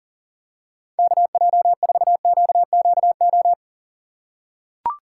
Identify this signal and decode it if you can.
Morse code, heard as a keyed tone, about 38 wpm, 700 Hz. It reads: KJ4QQO
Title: USA Call Signs - 6 Characters - 3x Character Spacing - 40wpm